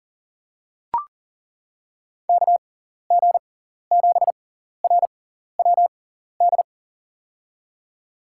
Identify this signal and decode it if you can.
Morse code, heard as a keyed tone, about 40 wpm, spaced as for 14 wpm, 700 Hz. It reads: KG7RWD
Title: USA Call Signs - 6 Characters - 6x Character Spacing - 40wpm